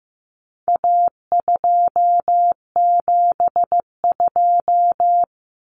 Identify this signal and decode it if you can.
Morse code, heard as a keyed tone, about 15 wpm, 700 Hz. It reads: A272
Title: Common Cut Numbers - 4-Digit 15wpm